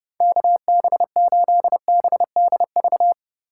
Morse code, heard as a keyed tone, about 31 wpm, 700 Hz. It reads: KB8BDV